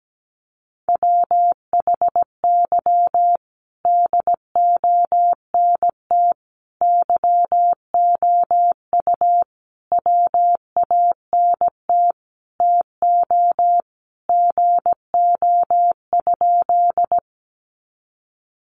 Morse code, heard as a keyed tone, about 17 wpm, 700 Hz. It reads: WHY DONT YOU WANT TO GO?
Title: Sentences from Top 300 Words 17wpm